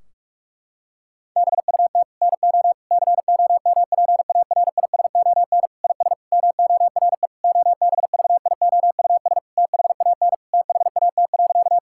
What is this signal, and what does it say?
Morse code, heard as a keyed tone, about 46 wpm, 700 Hz. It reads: BUT NO COMPARISON IS MORE OBVIOUS THAN THAT.
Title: Sentences from Top 1600 Words 45wpm